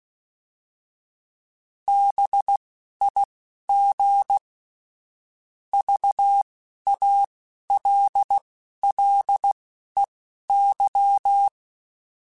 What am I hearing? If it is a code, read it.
Morse code, heard as a keyed tone, about 16 wpm, spaced as for 11 wpm, 800 Hz. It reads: BIG VALLEY
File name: fromSMs / FromSMs mp3s / Cities@10